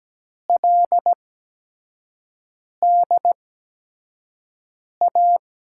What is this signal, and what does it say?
Morse code, heard as a keyed tone, about 17 wpm, spaced as for 5 wpm, 700 Hz. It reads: LDA
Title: USA Call Signs - 6 Characters - 8x Character Spacing - 17wpm